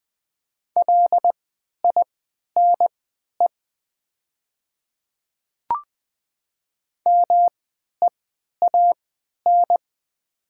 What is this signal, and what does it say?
Morse code, heard as a keyed tone, about 20 wpm, spaced as for 11 wpm, 700 Hz. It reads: LINE MEAN
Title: 3x Character Spacing - Top 200 Words - 20wpm